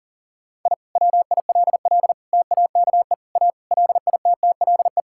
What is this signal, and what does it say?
Morse code, heard as a keyed tone, about 37 wpm, 700 Hz. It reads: I WILL TAKE A LITTLE
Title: Sentences from Top 1000 Words - Review Part 5 - 40wpm